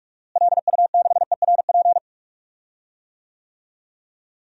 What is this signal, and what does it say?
Morse code, heard as a keyed tone, about 46 wpm, 700 Hz. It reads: LU6ERP
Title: DXCC-100 Call Signs - 6 Characters - Repeated 2x - 45wpm